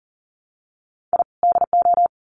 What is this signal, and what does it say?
Morse code, heard as a keyed tone, about 40 wpm, 700 Hz. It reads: I DO
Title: Sets of 2 Words 40wpm